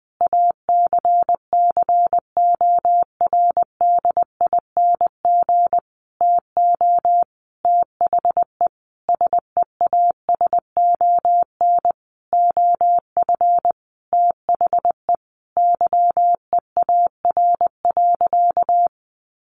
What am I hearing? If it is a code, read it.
Morse code, heard as a keyed tone, about 20 wpm, 700 Hz. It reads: ACCORDING TO THE SEASON OF THE YEAR.
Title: Sentences from Top 1400 Words 20wpm